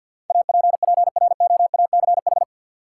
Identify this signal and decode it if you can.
Morse code, heard as a keyed tone, about 53 wpm, 700 Hz. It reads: APPROACH
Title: Sentences from Top 1400 Words - New Word Drill Alt - 50wpm